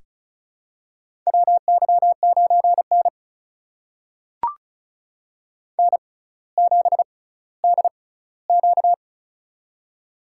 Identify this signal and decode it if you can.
Morse code, heard as a keyed tone, about 34 wpm, spaced as for 18 wpm, 700 Hz. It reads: WY9N N7DQ
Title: USA Call Signs - 4 Characters - 6x Character Spacing - 35wpm